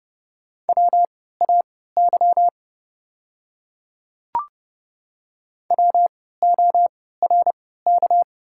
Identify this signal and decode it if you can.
Morse code, heard as a keyed tone, about 30 wpm, spaced as for 17 wpm, 700 Hz. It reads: WAY WORK
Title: 3x Character Spacing - Top 200 Words - 30wpm